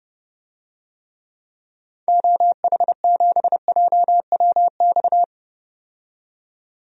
Morse code, heard as a keyed tone, about 30 wpm, 700 Hz. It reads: OH7JWX
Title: DXCC-100 Call Signs - 6 Characters - 30wpm